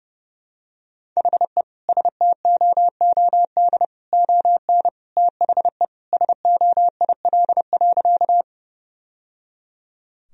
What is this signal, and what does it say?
Morse code, heard as a keyed tone, about 30 wpm, 700 Hz. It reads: HE STOOD ON THE SOIL.
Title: Sentences from Top 1000 Words - Review Part 4 - 30wpm